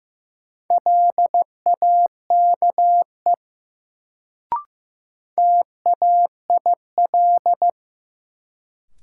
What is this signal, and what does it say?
Morse code, heard as a keyed tone, about 15 wpm, 700 Hz. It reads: LAKE TAIL
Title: Sentences from Top 1000 Words - Review Part 3 - New Word Drill - 15wpm